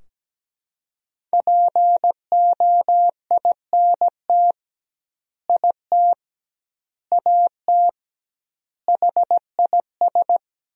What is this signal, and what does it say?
Morse code, heard as a keyed tone, about 17 wpm, 700 Hz. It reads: POINT IT AT HIS
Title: 2x Word-Spacing - Sentences from Top 300 Words - 17wpm